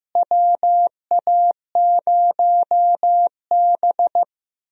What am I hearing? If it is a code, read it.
Morse code, heard as a keyed tone, about 15 wpm, 700 Hz. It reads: WA0B
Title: USA Call Signs - 4 Characters - Repeated 3x - 15wpm